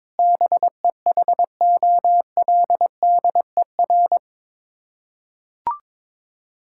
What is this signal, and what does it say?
Morse code, heard as a keyed tone, about 22 wpm, 700 Hz. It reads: BEHOLDER
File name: practice / 22wpm / Sets of 7 Words - 22wpm